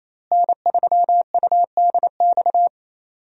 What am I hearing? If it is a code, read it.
Morse code, heard as a keyed tone, about 27 wpm, 700 Hz. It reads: N3UDX
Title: USA Call Signs - 5 Characters - Repeated 3x - 28wpm